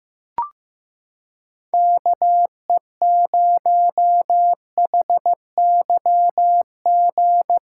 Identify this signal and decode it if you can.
Morse code, heard as a keyed tone, about 15 wpm, 700 Hz. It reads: KE0HYG
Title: USA Callsign DE Callsign - 6 Characters - 15wpm